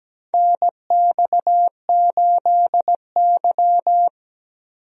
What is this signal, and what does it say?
Morse code, heard as a keyed tone, about 17 wpm, 700 Hz. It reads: NX8Y
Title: USA Call Signs - 4 Characters - 17wpm